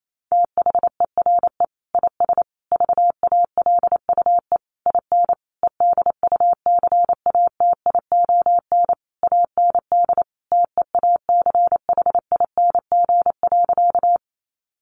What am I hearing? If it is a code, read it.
Morse code, heard as a keyed tone, about 28 wpm, 700 Hz. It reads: THERE IS VALUE IN EDUCATION AND TEACHING.
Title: Sentences from Top 1100 Words 28wpm